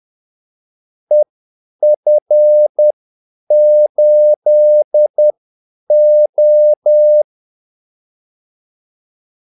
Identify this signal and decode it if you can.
Morse code, heard as a keyed tone, about 10 wpm, spaced as for 8 wpm, 600 Hz. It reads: EF8O
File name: Callsigns-10-8wpm